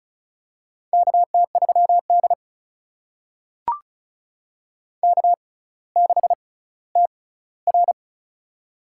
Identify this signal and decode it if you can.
Morse code, heard as a keyed tone, about 36 wpm, spaced as for 18 wpm, 700 Hz. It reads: KT3D K6TR
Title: USA Call Signs - 4 Characters - 6x Character Spacing - 35wpm